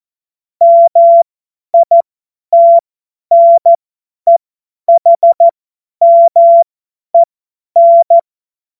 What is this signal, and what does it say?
Morse code, heard as a keyed tone, about 14 wpm, spaced as for 10 wpm, 680 Hz. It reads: MITNEHMEN
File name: Morsekurs / Witze / Witz 04. Januar 2022, 50 F70